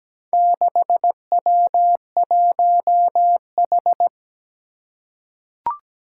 Morse code, heard as a keyed tone, about 17 wpm, 700 Hz. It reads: 6W1H
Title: Four Challenging Letter-Number Combinations 17wpm